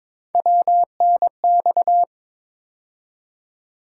Morse code, heard as a keyed tone, about 23 wpm, 700 Hz. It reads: WNX
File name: practice / 22wpm / Three Letter-Number Combinations - 22wpm